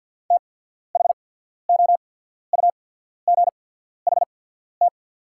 Morse code, heard as a keyed tone, about 48 wpm, spaced as for 14 wpm, 700 Hz. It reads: THOUGHT
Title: 8x Character Spacing - Top 300 Words - 50wpm